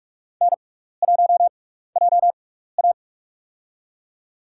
Morse code, heard as a keyed tone, about 46 wpm, spaced as for 16 wpm, 700 Hz. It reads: N1JA
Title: USA Call Signs - 4 Characters - 6x Character Spacing - 45wpm